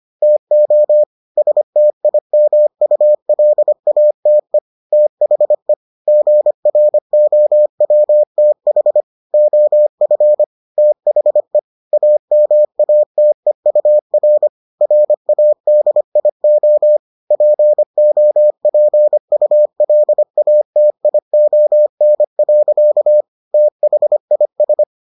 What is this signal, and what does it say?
Morse code, heard as a keyed tone, about 25 wpm, 600 Hz. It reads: TO STIMULATE THE GROWTH OF THE AMATEUR RADIO POPULATION. THIS